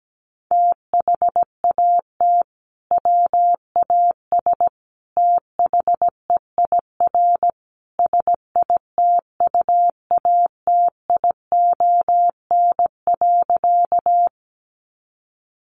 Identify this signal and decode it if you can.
Morse code, heard as a keyed tone, about 17 wpm, 700 Hz. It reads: THAT WAS THEIR SITUATION.
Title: Sentences from Top 1200 Words 17wpm